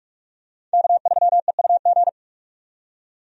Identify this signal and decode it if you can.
Morse code, heard as a keyed tone, about 46 wpm, 700 Hz. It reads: K3EUG
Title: USA Call Signs - 5 Characters - 45wpm